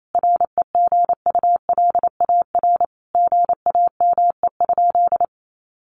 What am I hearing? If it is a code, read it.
Morse code, heard as a keyed tone, about 28 wpm, 700 Hz. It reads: REGULAR GAME?
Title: Sentences from Top 900 Words 28wpm